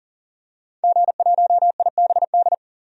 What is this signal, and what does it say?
Morse code, heard as a keyed tone, about 40 wpm, 700 Hz. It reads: G1IBD
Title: DXCC-100 Call Signs - 5 Characters - 40wpm